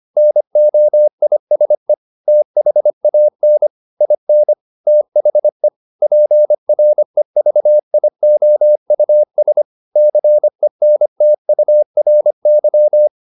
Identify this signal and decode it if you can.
Morse code, heard as a keyed tone, about 25 wpm, 600 Hz. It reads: NOISE THAN IN THE PREVIOUS CENTURY